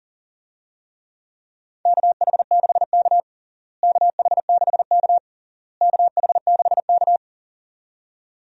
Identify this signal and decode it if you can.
Morse code, heard as a keyed tone, about 40 wpm, 700 Hz. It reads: KH6K KH6K KH6K
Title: DXCC-100 Call Signs - 4 Characters - Repeated 3x - 40wpm